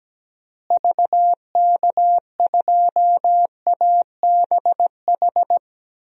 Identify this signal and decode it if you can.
Morse code, heard as a keyed tone, about 17 wpm, 700 Hz. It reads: VK2ABH
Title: DXCC-100 Call Signs - 6 Characters - Repeated 2x - 17wpm